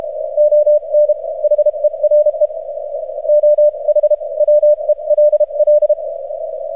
Morse code, heard as a keyed tone, about 32 wpm, spaced as for 27 wpm, 600 Hz. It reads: ON HERE OH WELL
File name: archived / CRSnet